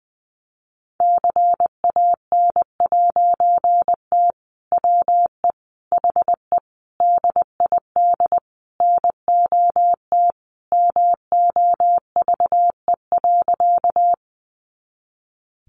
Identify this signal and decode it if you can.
Morse code, heard as a keyed tone, about 20 wpm, 700 Hz. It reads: CAN'T WE HE DID NOT MOVE.
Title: A Sound Of Thunder 20wpm